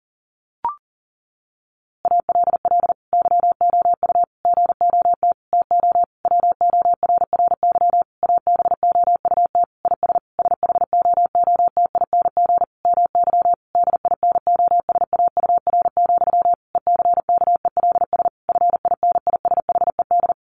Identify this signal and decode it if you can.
Morse code, heard as a keyed tone, about 40 wpm, 700 Hz. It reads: ALL YOU GOT TO WORRY ABOUT IS SHOOTING MY DINOSAUR, ECKELS FINISHED
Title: A Sound Of Thunder 40wpm